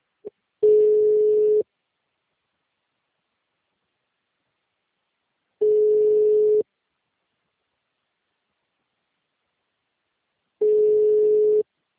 Ringback tone (Europe, 425 Hz).